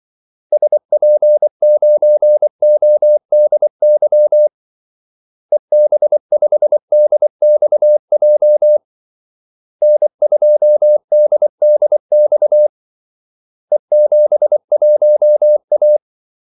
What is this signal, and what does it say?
Morse code, heard as a keyed tone, about 24 wpm, 600 Hz. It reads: SP9ODY EB5DXJ N2DDX E71A
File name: Callsigns-24wpm